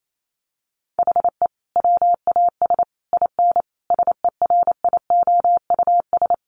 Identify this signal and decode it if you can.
Morse code, heard as a keyed tone, about 28 wpm, 700 Hz. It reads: HE WAS IN SERIOUS
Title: Sentences from Top 1300 Words 28wpm